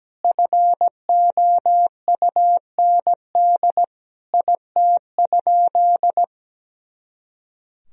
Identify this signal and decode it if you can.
Morse code, heard as a keyed tone, about 17 wpm, 700 Hz. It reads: FOUND IT?
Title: Sentences from Top 300 Words and Abbreviations 17wpm